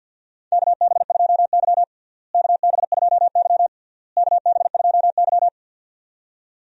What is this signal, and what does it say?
Morse code, heard as a keyed tone, about 52 wpm, 700 Hz. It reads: KB2Y KB2Y KB2Y
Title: USA Call Signs - 4 Characters - Repeated 3x - 50wpm